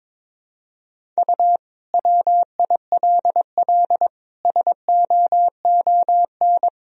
Morse code, heard as a keyed tone, about 22 wpm, 700 Hz. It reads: U WILL SOON
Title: Sentences from Top 400 Words and Abbreviations 22wpm